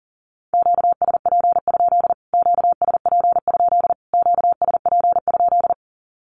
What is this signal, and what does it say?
Morse code, heard as a keyed tone, about 40 wpm, 700 Hz. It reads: QSP? QSP? QSP?
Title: Q-Codes Repeated 3x 40wpm